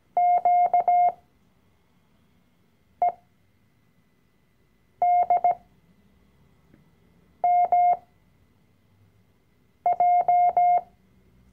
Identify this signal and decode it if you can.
Morse code, heard as a keyed tone, about 17 wpm, spaced as for 4 wpm, 710 Hz. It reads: QEDMJ
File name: CW-Online-Kurs / Anfänger / Lektion 012